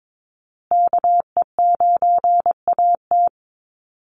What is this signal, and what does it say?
Morse code, heard as a keyed tone, about 22 wpm, 700 Hz. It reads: KE9AT
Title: USA Call Signs Repeated 2x 22wpm